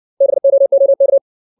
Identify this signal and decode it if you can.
Morse code, heard as a keyed tone, about 60 wpm, 550 Hz. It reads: BOOK
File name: 632 words / 632 Words 60WPM